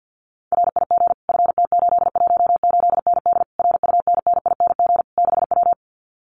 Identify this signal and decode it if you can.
Morse code, heard as a keyed tone, about 50 wpm, 700 Hz. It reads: RIG FT818ND RUNNING 6W
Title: Common QSO Phrases Encore 50wpm